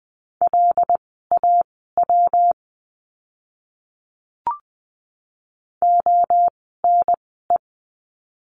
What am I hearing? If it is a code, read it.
Morse code, heard as a keyed tone, about 20 wpm, spaced as for 15 wpm, 700 Hz. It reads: LAW ONE
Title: 115 Lesson 19f - Letter w - Words 20wpm